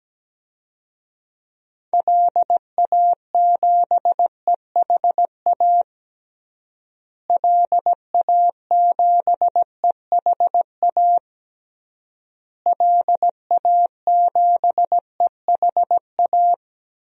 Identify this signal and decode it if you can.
Morse code, heard as a keyed tone, about 17 wpm, 700 Hz. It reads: LA7EHA LA7EHA LA7EHA
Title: DXCC-100 Call Signs - 6 Characters - Repeated 3x - 17wpm